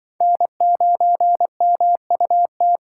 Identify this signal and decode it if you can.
Morse code, heard as a keyed tone, about 23 wpm, 700 Hz. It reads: N9MUT
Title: Speed Racing - USA 1x3 Call Signs 20wpm